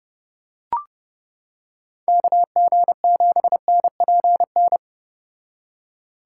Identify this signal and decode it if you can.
Morse code, heard as a keyed tone, about 29 wpm, 700 Hz. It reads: KG7NPN